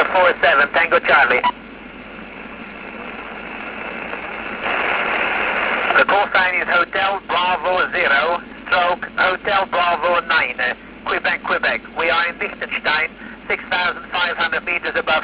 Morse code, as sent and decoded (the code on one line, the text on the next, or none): none